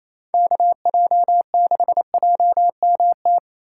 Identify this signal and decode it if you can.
Morse code, heard as a keyed tone, about 28 wpm, 700 Hz. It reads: KJ6JMT
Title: USA Call Signs - 6 Characters - 2x Character Spacing - 28wpm